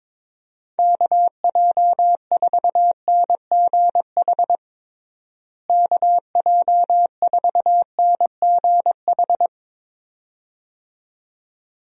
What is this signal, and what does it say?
Morse code, heard as a keyed tone, about 22 wpm, 700 Hz. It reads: KJ4NGH KJ4NGH
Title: USA Call Signs - 6 Characters - Repeated 2x - 22wpm